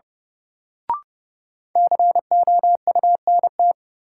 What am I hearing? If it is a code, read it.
Morse code, heard as a keyed tone, about 30 wpm, 700 Hz. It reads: COUNT
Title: Sentences from Top 1500 Words - New Word Drill - 30wpm